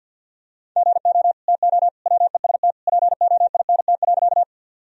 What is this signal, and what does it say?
Morse code, heard as a keyed tone, about 51 wpm, 700 Hz. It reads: GO TO WEST POINT.
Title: Sentences from Top 1000 Words - Review Part 2 - 50wpm